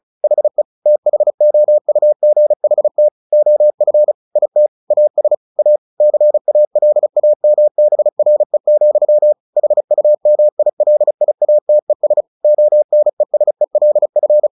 Morse code, heard as a keyed tone, about 35 wpm, 600 Hz. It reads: HE THOUGHT OF IT AS A CALAMBRE, HUMILIATES ONESELF
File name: the old man and the sea / the old man and the sea 1 35wpm